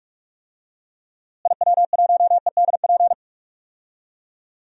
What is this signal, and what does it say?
Morse code, heard as a keyed tone, about 48 wpm, 700 Hz. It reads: IW1EDP
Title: DXCC-100 Call Signs - 6 Characters - Repeated 3x - 45wpm